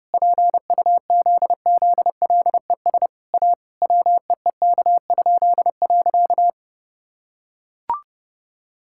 Morse code, heard as a keyed tone, about 28 wpm, 700 Hz. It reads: PUZZLES A WEEK?.
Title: Pangram Challenge v2 - 30wpm